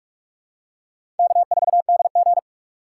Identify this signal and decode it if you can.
Morse code, heard as a keyed tone, about 44 wpm, 700 Hz. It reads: K4DG
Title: USA Call Signs - 4 Characters - 6x Character Spacing - 45wpm